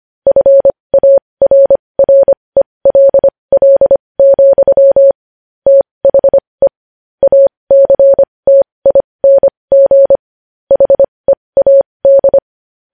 Morse code, heard as a keyed tone, about 25 wpm, spaced as for 20 wpm, 550 Hz. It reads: FARRELL, THE ACTING HEAD
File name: podcast / News headlines-1x-25x20